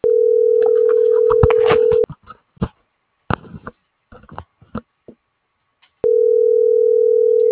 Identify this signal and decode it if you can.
Ringback tone (North America), 440 + 480 Hz.